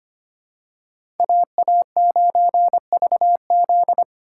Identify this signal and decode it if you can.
Morse code, heard as a keyed tone, about 25 wpm, 700 Hz. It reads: AA9VZ
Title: USA Call Signs - 5 Characters - 2x Character Spacing - 25wpm